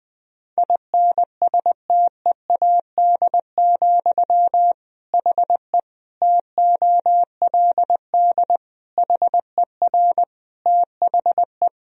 Morse code, heard as a keyed tone, about 21 wpm, 700 Hz. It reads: INSTEAD, HE TOLD HER THE
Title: Sentences from Top 1000 Words - Review Part 4 - 20wpm